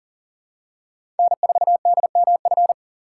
Morse code, heard as a keyed tone, about 40 wpm, 700 Hz. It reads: N4DMF
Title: USA Call Signs - 5 Characters - 3x Character Spacing - 40wpm